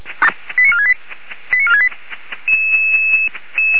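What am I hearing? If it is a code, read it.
Five-tone selective call (ZVEI), heard as quick runs of five tones, tones of about 70 ms. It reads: 89468 89468
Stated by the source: ZVEI five-tone call received by SDR.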